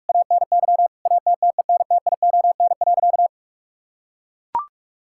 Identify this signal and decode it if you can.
Morse code, heard as a keyed tone, about 43 wpm, 700 Hz. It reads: ANY ATTENTION.
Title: Sentences from Top 1000 Words - Review Part 6 - 45wpm